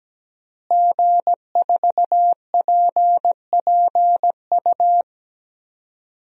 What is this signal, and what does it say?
Morse code, heard as a keyed tone, about 17 wpm, 700 Hz. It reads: G4PPU
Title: DXCC-100 Call Signs - 5 Characters - Repeated 2x - 17wpm